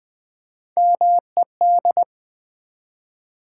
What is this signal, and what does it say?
Morse code, heard as a keyed tone, about 20 wpm, 700 Hz. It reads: MED
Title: Build A Word - 1 Letter At A Time - 5-Letter Words - 20wpm